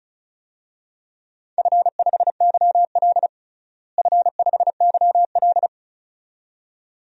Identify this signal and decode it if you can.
Morse code, heard as a keyed tone, about 36 wpm, 700 Hz. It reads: F5YL F5YL
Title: DXCC-100 Call Signs - 4 Characters - Repeated 2x - 35wpm